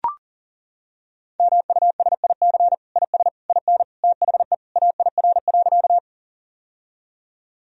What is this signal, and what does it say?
Morse code, heard as a keyed tone, about 37 wpm, 700 Hz. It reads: MUSIC IS IN THE AIR.
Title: Sentences from Top 1000 Words - Review Part 1 - 40wpm